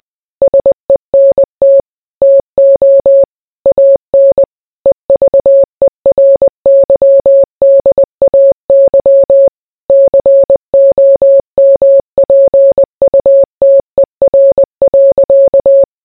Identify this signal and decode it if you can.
Morse code, heard as a keyed tone, about 20 wpm, 550 Hz. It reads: SENT TO AN EVERYDAY COMPUTER.